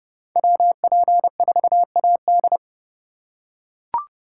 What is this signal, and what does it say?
Morse code, heard as a keyed tone, about 30 wpm, 700 Hz. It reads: WP4AD